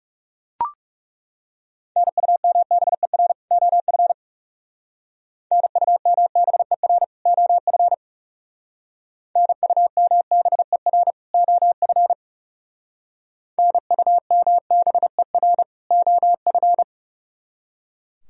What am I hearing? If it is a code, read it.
Morse code, heard as a keyed tone, about 36 wpm, 700 Hz. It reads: NUMBER OF NUMBER OF NUMBER OF NUMBER OF
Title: Speed-Racing - Sets of 2 Words 30wpm